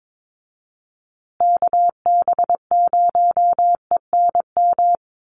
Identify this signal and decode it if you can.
Morse code, heard as a keyed tone, about 22 wpm, 700 Hz. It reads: KB0ENM
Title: USA Call Signs Repeated 2x 22wpm